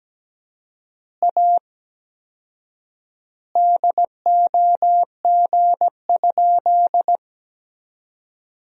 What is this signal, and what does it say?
Morse code, heard as a keyed tone, about 17 wpm, 700 Hz. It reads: A DOG?
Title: 4x Word-Spacing - Sentences from Top 400 Words - 17wpm